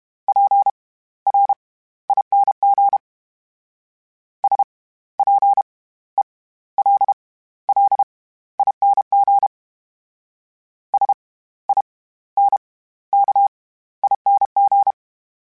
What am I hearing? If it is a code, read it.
Morse code, heard as a keyed tone, about 32 wpm, spaced as for 16 wpm, 800 Hz. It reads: PRING SPELLING SINKING
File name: Special INGs / Special INGs-mp3s / SlowToFaster ing @30